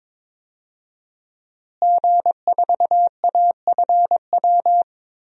Morse code, heard as a keyed tone, about 21 wpm, 700 Hz. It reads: G4AFW
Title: DXCC-100 Call Signs - 5 Characters - Repeated 3x - 22wpm